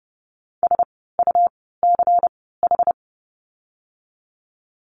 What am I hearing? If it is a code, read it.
Morse code, heard as a keyed tone, about 30 wpm, spaced as for 17 wpm, 700 Hz. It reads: SUCH
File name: course / 30:10wpm / 108 Lesson 18f - Letter m - Words 30wpm